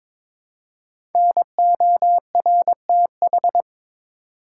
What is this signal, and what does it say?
Morse code, heard as a keyed tone, about 22 wpm, 700 Hz. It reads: NORTH
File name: 5 Letter Words - First 3 Letters Given - 22wpm